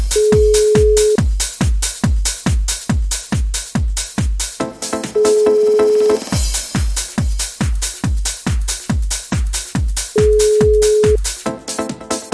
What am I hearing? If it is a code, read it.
Ringback tone (Europe, 425 Hz).